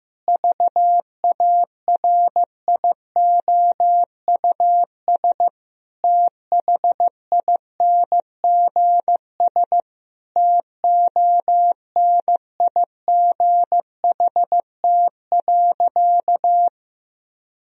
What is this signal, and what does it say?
Morse code, heard as a keyed tone, about 15 wpm, 700 Hz. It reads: VARIOUS THINGS TONIGHT.
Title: Sentences from Top 1000 Words - Review Part 4 - 15wpm